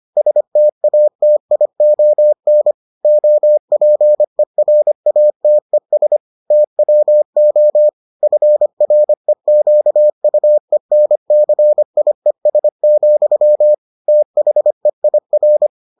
Morse code, heard as a keyed tone, about 25 wpm, 600 Hz. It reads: STATION OPERATES TWO FREQUENCIES, THEIR